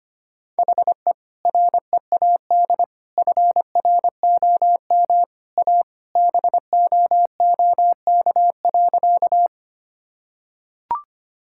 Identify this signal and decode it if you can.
Morse code, heard as a keyed tone, about 25 wpm, 700 Hz. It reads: HE READ FROM A BOOK.
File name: practice / 25wpm / Sentences with 2-Word Phrases - 25wpm